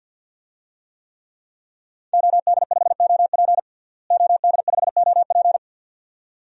Morse code, heard as a keyed tone, about 51 wpm, 700 Hz. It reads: OD5OP OD5OP
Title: DXCC-100 Call Signs - 5 Characters - Repeated 2x - 50wpm